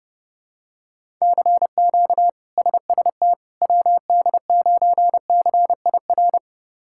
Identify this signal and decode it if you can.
Morse code, heard as a keyed tone, about 30 wpm, 700 Hz. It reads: CQ SST WD9CIR
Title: K1USN SST QSO Practice - 30wpm